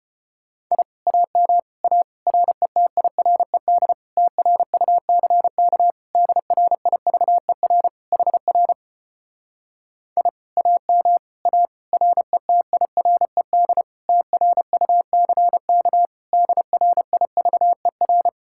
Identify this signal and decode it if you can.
Morse code, heard as a keyed tone, about 33 wpm, 700 Hz. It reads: I AM A RETIRED TRUCK DRIVER HR I AM A RETIRED TRUCK DRIVER